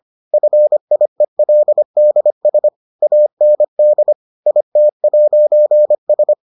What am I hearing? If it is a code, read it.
Morse code, heard as a keyed tone, about 25 wpm, 600 Hz. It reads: FIELDS AND IT'S